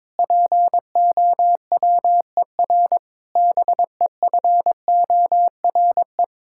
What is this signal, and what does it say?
Morse code, heard as a keyed tone, about 22 wpm, 700 Hz. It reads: POWER BEFORE
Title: Sentences from Top 1000 Words - Review Part 6 - 22wpm